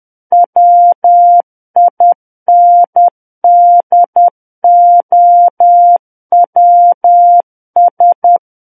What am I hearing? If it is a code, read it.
Morse code, heard as a keyed tone, about 10 wpm, 700 Hz. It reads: WINDOWS
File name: Book One / 01-Chapter 1 - 10 WPM